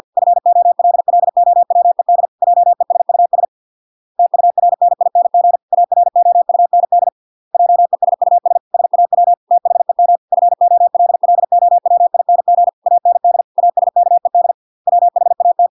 Morse code, heard as a keyed tone, about 50 wpm, 700 Hz. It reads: FOLLOWED JESUS TURNING AROUND JESUS SAW THEM FOLLOWING AND ASKED WHAT